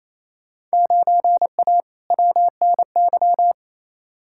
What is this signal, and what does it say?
Morse code, heard as a keyed tone, about 28 wpm, 700 Hz. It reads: 9A WNY
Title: Field Day Exchanges Encore - 28wpm